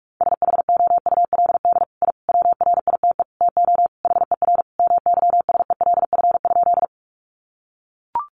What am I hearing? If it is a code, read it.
Morse code, heard as a keyed tone, about 44 wpm, 700 Hz. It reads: SHOULD I WRITE TO HER MYSELF?
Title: Sentences from Top 1300 Words 45wpm